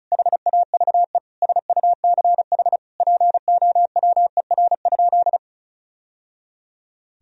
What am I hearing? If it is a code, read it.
Morse code, heard as a keyed tone, about 34 wpm, 700 Hz. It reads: HAVE SUCH POWER?
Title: Sentences from Top 1000 Words - Review Part 2 - 35wpm